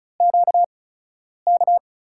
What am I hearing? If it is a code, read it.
Morse code, heard as a keyed tone, about 36 wpm, spaced as for 10 wpm, 700 Hz. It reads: QK
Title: USA Call Signs - 5 Characters - 8x Character Spacing - 35wpm